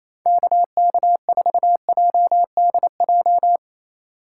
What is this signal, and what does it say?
Morse code, heard as a keyed tone, about 28 wpm, 700 Hz. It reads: KK4JDJ
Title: USA Call Signs - 4x Character Spacing - 28wpm